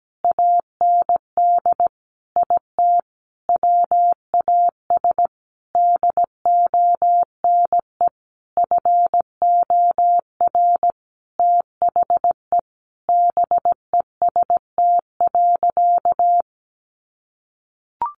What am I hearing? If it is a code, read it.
Morse code, heard as a keyed tone, about 17 wpm, 700 Hz. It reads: AND IT WAS DONE FOR THE BEST.
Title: Sentences from Top 600 Words 17wpm